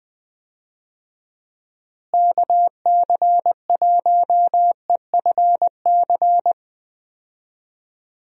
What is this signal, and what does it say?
Morse code, heard as a keyed tone, about 21 wpm, 700 Hz. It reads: KC1EFC
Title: USA Call Signs - 6 Characters - Repeated 3x - 20wpm